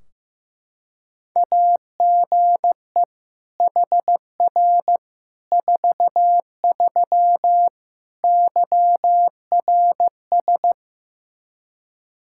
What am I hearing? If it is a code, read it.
Morse code, heard as a keyed tone, about 15 wpm, 700 Hz. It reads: AGE HR 43 YRS
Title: Practice QSO - Exchange 3 - More Info - 15wpm